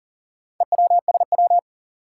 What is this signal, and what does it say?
Morse code, heard as a keyed tone, about 38 wpm, 700 Hz. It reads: EWSW
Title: Four Challenging Letter-Number Combinations 40wpm